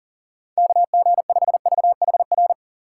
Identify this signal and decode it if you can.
Morse code, heard as a keyed tone, about 40 wpm, 700 Hz. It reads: KG5VHR
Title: USA Call Signs - 6 Characters - 3x Character Spacing - 40wpm